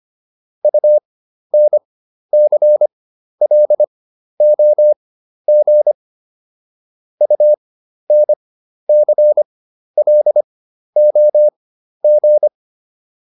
Morse code, heard as a keyed tone, about 25 wpm, spaced as for 12 wpm, 600 Hz. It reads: UNCLOG UNCLOG